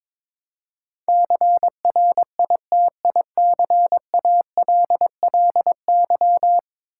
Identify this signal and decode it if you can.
Morse code, heard as a keyed tone, about 22 wpm, 700 Hz. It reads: CRITICALLY
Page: morse-code-ninja-practice-22wpm